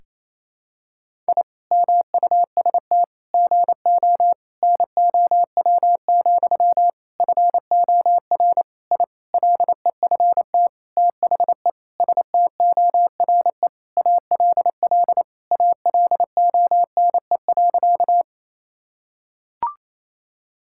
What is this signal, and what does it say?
Morse code, heard as a keyed tone, about 28 wpm, 700 Hz. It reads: I MUST GO NOW, FOR I LEFT THE STORE ALL ALONE.
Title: Sentences from Top 1000 Words - Review Part 5 - 28wpm